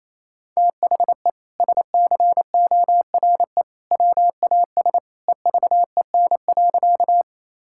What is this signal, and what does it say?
Morse code, heard as a keyed tone, about 28 wpm, 700 Hz. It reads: THE SCORE WAS EVEN.
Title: Sentences from Top 1600 Words 28wpm